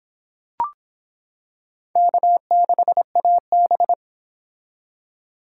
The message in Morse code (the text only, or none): K6AB